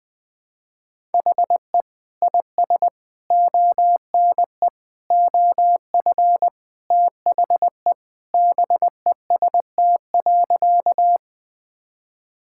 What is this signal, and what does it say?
Morse code, heard as a keyed tone, about 20 wpm, 700 Hz. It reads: HE IS ONE OF THE BEST.
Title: Sentences with 4-Word Phrases - 20wpm